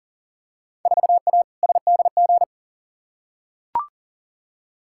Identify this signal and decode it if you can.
Morse code, heard as a keyed tone, about 40 wpm, 700 Hz. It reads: 4A SDG
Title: Field Day Exchanges - 40wpm